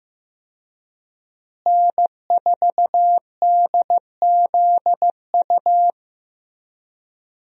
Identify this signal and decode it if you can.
Morse code, heard as a keyed tone, about 16 wpm, 700 Hz. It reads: N4DZU
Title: USA Call Signs - 5 Characters - Repeated 2x - 15wpm